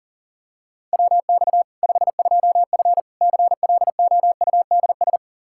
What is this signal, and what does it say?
Morse code, heard as a keyed tone, about 37 wpm, 700 Hz. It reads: WX 52F CLOUDS
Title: K1USN SST QSO Practice 40wpm